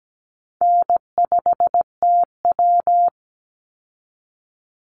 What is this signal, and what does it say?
Morse code, heard as a keyed tone, about 17 wpm, 700 Hz. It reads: N5TW
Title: Easy Call Signs Repeated 3x - 17wpm